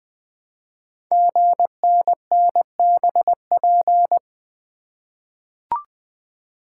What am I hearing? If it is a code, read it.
Morse code, heard as a keyed tone, about 20 wpm, 700 Hz. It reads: GNNBP
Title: Code Groups - 5 Letters - 20wpm